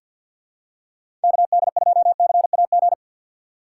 Morse code, heard as a keyed tone, about 49 wpm, 700 Hz. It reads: KD2CAG
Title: USA Call Signs - 6 Characters - 4x Character Spacing - 50wpm